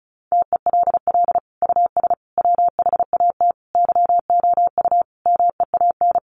Morse code, heard as a keyed tone, about 34 wpm, 700 Hz. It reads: TELL US WHAT YOU MEAN
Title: Sentences from Top 1000 Words 35wpm